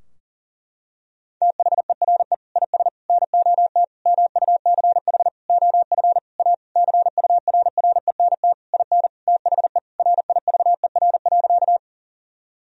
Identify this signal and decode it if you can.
Morse code, heard as a keyed tone, about 40 wpm, 700 Hz. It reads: THERE IS NOT MUCH OF A CURRENT IN THE RIVER.
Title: Sentences from Top 1000 Words - Review Part 4 - 40wpm